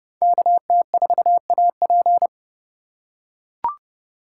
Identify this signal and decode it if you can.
Morse code, heard as a keyed tone, about 30 wpm, 700 Hz. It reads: KT4AP